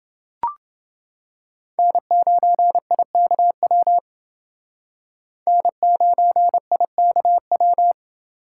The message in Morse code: N9IKW N9IKW

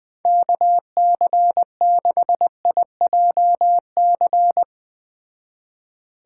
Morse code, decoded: KC6IJC

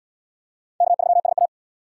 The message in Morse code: K3SA

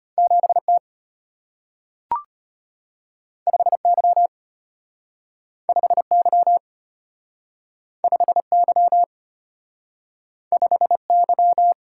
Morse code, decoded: 7T 5Y 5Y 5Y 5Y

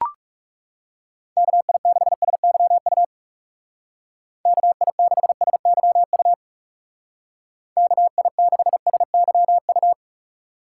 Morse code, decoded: KI6SYU KI6SYU KI6SYU

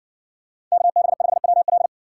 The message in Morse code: KB5WL

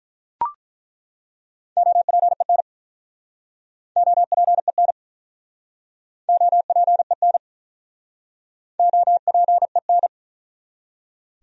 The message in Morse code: OPEN OPEN OPEN OPEN

